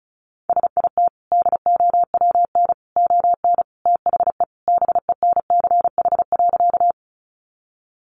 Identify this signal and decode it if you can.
Morse code, heard as a keyed tone, about 34 wpm, 700 Hz. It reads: SIT DOWN ON THE BENCH.
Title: Sentences from Top 1400 Words 35wpm